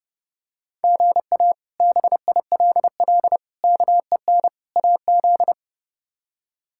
Morse code, heard as a keyed tone, about 30 wpm, 700 Hz. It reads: GA BILL KEN AZ
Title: K1USN SST QSO Practice - 30wpm